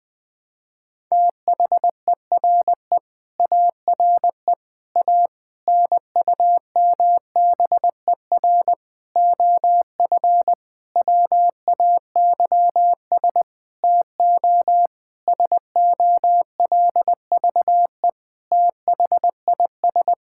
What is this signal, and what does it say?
Morse code, heard as a keyed tone, about 21 wpm, 700 Hz. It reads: THERE ARE A NUMBER OF WAYS TO SOLVE THIS
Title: Sentences with 7-Word Phrases - Spoken Beginning - 20wpm